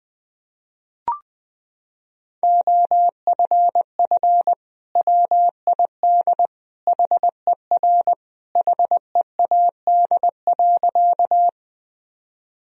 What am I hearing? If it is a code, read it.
Morse code, heard as a keyed tone, about 20 wpm, 700 Hz. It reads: OFF WID HER HEAD.
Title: Sentences from Top 300 Words and Abbreviations 20wpm